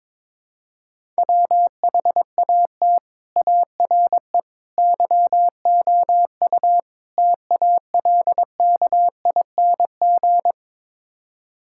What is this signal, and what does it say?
Morse code, heard as a keyed tone, about 22 wpm, 700 Hz. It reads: WHAT ARE YOU TALKING
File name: practice / 22wpm / Build A Phrase - 5-Word Phrases - 22wpm